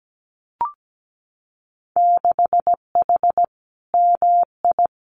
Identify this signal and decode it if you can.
Morse code, heard as a keyed tone, about 17 wpm, 700 Hz. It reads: 6H MI
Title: Winter Field Day Exchanges Encore 17wpm